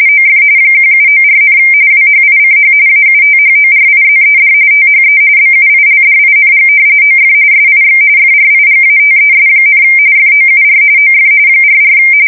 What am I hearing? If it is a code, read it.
RTTY, about 45.45 baud. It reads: OF 1794, WHICH AUTHORIZED THE BUILDING OF SIX FRIGATES, FOUR OF 44
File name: fsk170 / usnavy1